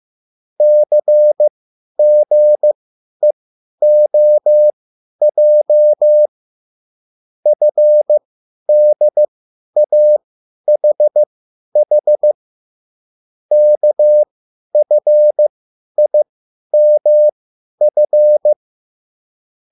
Morse code, heard as a keyed tone, about 15 wpm, spaced as for 10 wpm, 600 Hz. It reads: CGEOJ FDAHH KFIMF